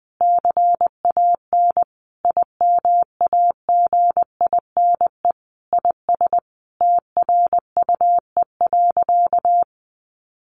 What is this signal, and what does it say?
Morse code, heard as a keyed tone, about 20 wpm, 700 Hz. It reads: CAN IMAGINE IS TRUE.
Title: Sentences from Top 1400 Words 20wpm